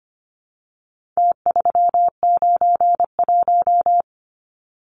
Morse code, heard as a keyed tone, about 25 wpm, 700 Hz. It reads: T391
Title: Numbers - Contest Style - 4-Digit 25wpm